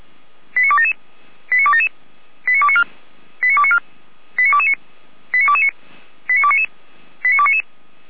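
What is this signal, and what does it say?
Five-tone selective call (ZVEI), heard as quick runs of five tones, tones of about 70 ms. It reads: 89277 89277 89294 89294 89229 89229 89299 89299
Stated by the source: ZVEI five-tone selective call received on SDR.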